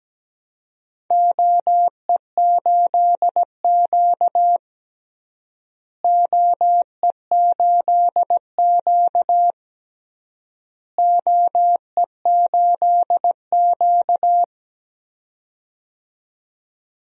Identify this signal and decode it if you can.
Morse code, heard as a keyed tone, about 17 wpm, 700 Hz. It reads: OE8Q OE8Q OE8Q
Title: DXCC-100 Call Signs - 4 Characters - Repeated 3x - 17wpm